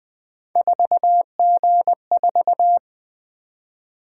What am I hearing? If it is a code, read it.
Morse code, heard as a keyed tone, about 20 wpm, 700 Hz. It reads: 4G4
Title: Three Letter-Number Combinations - 20wpm